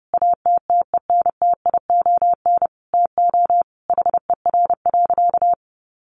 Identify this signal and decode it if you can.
Morse code, heard as a keyed tone, about 30 wpm, 700 Hz. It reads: ATTENTION TO HER.
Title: Sentences from Top 1000 Words 30wpm